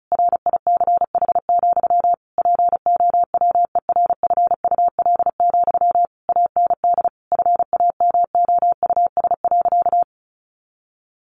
Morse code, heard as a keyed tone, about 34 wpm, 700 Hz. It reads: RICH, POWERFUL, AND FAMOUS.
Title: Sentences from Top 1500 Words 35wpm